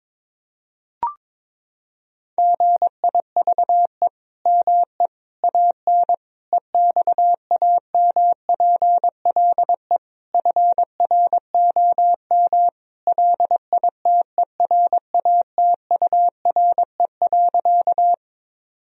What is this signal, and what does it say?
Morse code, heard as a keyed tone, about 22 wpm, 700 Hz. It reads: GIVE ME AN EXAMPLE FROM LITERATURE.
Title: Sentences from Top 1000 Words - Review Part 3 - 22wpm